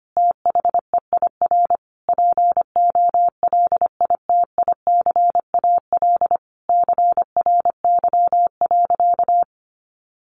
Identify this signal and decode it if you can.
Morse code, heard as a keyed tone, about 25 wpm, 700 Hz. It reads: THEIR POLITICAL CRY.
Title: Sentences from Top 1400 Words 25wpm